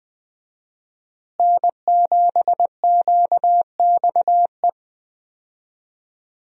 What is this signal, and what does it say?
Morse code, heard as a keyed tone, about 20 wpm, 700 Hz. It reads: N7QXE
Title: USA Call Signs - 5 Characters - Repeated 2x - 20wpm